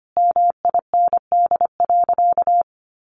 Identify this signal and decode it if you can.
Morse code, heard as a keyed tone, about 25 wpm, 700 Hz. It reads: MIND.
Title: Sentences from Top 1100 Words 25wpm